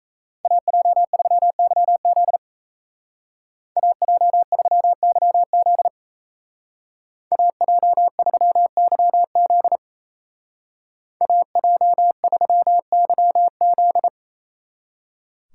Morse code, heard as a keyed tone, about 33 wpm, 700 Hz. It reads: AJ3YZ AJ3YZ AJ3YZ AJ3YZ
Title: Speed Racing - USA 2x2 Call Signs 28wpm